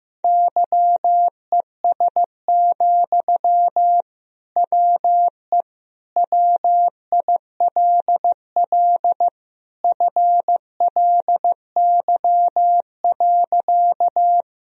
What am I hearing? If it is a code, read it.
Morse code, heard as a keyed tone, about 15 wpm, 700 Hz. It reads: YES, WE WILL FLY.
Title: Sentences from Top 1000 Words - Review Part 2 - 15wpm